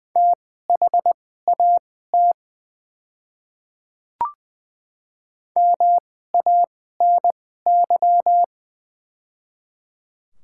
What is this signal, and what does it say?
Morse code, heard as a keyed tone, about 20 wpm, spaced as for 14 wpm, 700 Hz. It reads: THAT MANY
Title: 2x Character Spacing - Top 300 Words - 20wpm